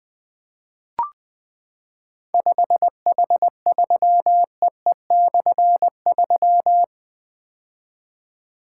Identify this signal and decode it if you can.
Morse code, heard as a keyed tone, about 20 wpm, 700 Hz. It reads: 5H3EE/3
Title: Difficult Call Signs 20wpm